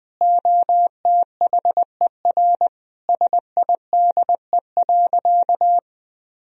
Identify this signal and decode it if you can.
Morse code, heard as a keyed tone, about 20 wpm, 700 Hz. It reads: OTHER SIDE.
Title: Sentences from Top 1000 Words - Review Part 1 - 20wpm